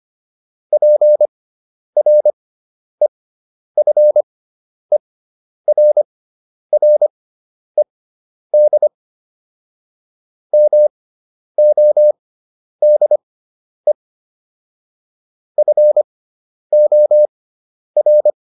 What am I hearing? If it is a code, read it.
Morse code, heard as a keyed tone, about 25 wpm, spaced as for 10 wpm, 600 Hz. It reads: PREFERRED MODE FOR